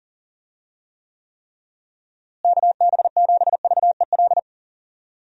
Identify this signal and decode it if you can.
Morse code, heard as a keyed tone, about 40 wpm, 700 Hz. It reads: KB7VEL
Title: USA Call Signs - 6 Characters - Repeated 3x - 40wpm